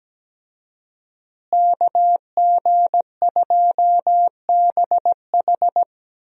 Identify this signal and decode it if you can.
Morse code, heard as a keyed tone, about 17 wpm, 700 Hz. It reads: KG2BH
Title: USA Call Signs - 5 Characters - Repeated 3x - 17wpm